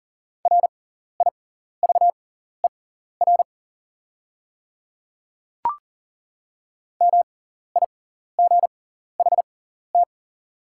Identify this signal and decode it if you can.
Morse code, heard as a keyed tone, about 40 wpm, spaced as for 14 wpm, 690 Hz. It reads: RIVER MIGHT
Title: 6x Character Spacing - Top 300 Words - 40wpm